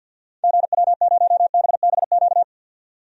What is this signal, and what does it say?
Morse code, heard as a keyed tone, about 52 wpm, 700 Hz. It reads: GW0BBQ
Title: DXCC-100 Call Signs - 6 Characters - 50wpm